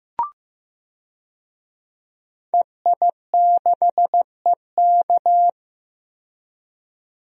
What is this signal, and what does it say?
Morse code, heard as a keyed tone, about 15 wpm, 700 Hz. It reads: EI6EK
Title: DXCC-100 Call Signs - 5 Characters - 15wpm